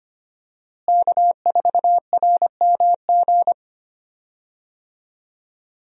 Morse code, heard as a keyed tone, about 25 wpm, 700 Hz. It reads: K4RMG
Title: USA Call Signs - 5 Characters - Repeated 3x - 25wpm